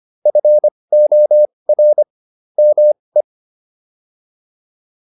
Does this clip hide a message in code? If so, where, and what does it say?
Yes: Morse code from 0.3 to 3.2 s, heard as a keyed tone, about 25 wpm, spaced as for 20 wpm, 600 Hz. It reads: FOR ME